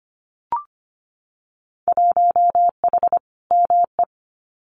Morse code, heard as a keyed tone, about 25 wpm, 700 Hz. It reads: 1H ME
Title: Winter Field Day Exchanges 25wpm